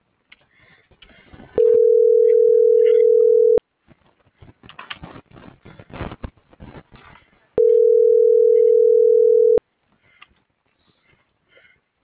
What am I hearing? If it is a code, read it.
Ringback tone (North America), 440 + 480 Hz.